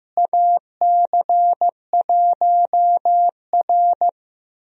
Morse code, heard as a keyed tone, about 15 wpm, 700 Hz. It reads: AC1R
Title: USA Call Signs - 4 Characters - Repeated 3x - 15wpm